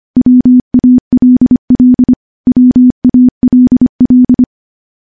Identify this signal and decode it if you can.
Morse code, heard as a keyed tone, about 25 wpm, 250 Hz. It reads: WALL WALL
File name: morse-training-25WPM